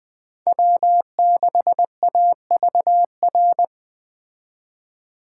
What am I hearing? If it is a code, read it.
Morse code, heard as a keyed tone, about 20 wpm, 700 Hz. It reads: W6AVR
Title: Speed Racing - USA 1x3 Call Signs 17wpm